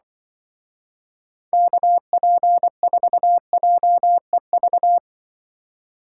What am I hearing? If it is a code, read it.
Morse code, heard as a keyed tone, about 24 wpm, 700 Hz. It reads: KP4JEV
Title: Speed Racing - USA 2x3 Call Signs 20wpm